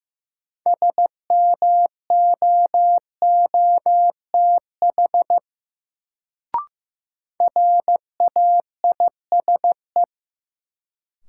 Morse code, heard as a keyed tone, about 15 wpm, 700 Hz. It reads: SMOOTH RAISE
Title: Sentences from Top 1400 Words - New Word Drill Alt - 15wpm